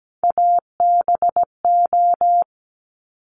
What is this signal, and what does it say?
Morse code, heard as a keyed tone, about 17 wpm, 690 Hz. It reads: ABO
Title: Top Three Letter Combinations in Words 17wpm